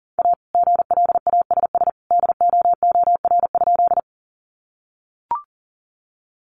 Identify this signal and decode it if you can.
Morse code, heard as a keyed tone, about 40 wpm, 700 Hz. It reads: A GLASS DOOR?
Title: Sentences from Top 1300 Words 40wpm